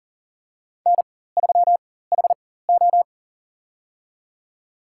Morse code, heard as a keyed tone, about 40 wpm, spaced as for 19 wpm, 700 Hz. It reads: N3HO
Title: USA Call Signs - 4 Characters - 4x Character Spacing - 40wpm